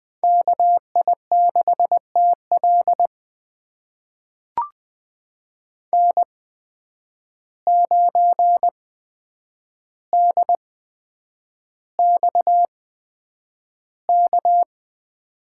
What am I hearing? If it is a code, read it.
Morse code, heard as a keyed tone, about 20 wpm, spaced as for 9 wpm, 700 Hz. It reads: KI6TL N9DXK
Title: USA Call Signs - 5 Characters - 8x Character Spacing - 20wpm